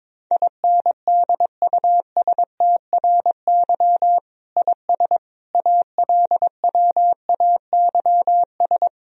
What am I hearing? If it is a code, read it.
Morse code, heard as a keyed tone, about 22 wpm, 700 Hz. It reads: INDUSTRY IS ALWAYS
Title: Sentences from Top 1000 Words - Review Part 4 - 22wpm